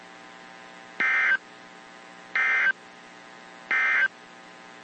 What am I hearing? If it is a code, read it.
Emergency Alert System (SAME) tones, heard as harsh buzzing bursts. It is heard from 0.9 to 4.2 s.
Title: WZ2523 - NOAA Weather Radio